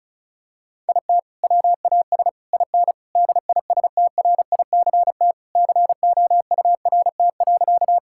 Morse code, heard as a keyed tone, about 36 wpm, 700 Hz. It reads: IT WAS IN DISTRICT COURT.
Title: Sentences from Top 1600 Words 35wpm